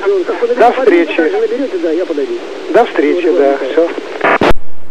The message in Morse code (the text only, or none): none